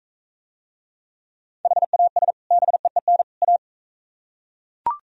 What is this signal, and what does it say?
Morse code, heard as a keyed tone, about 45 wpm, 700 Hz. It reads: HAS BEEN A